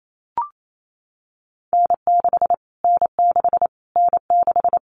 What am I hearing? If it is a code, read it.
Morse code, heard as a keyed tone, about 28 wpm, 700 Hz. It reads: N6 N6 N6
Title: Callsign Prefixes for Top 25 DXCC - Repeated 3x - 28wpm